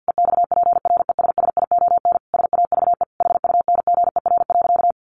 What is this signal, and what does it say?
Morse code, heard as a keyed tone, about 50 wpm, 700 Hz. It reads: EXPRESSION SAVE HUNGER.
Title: A Sound Of Thunder 50wpm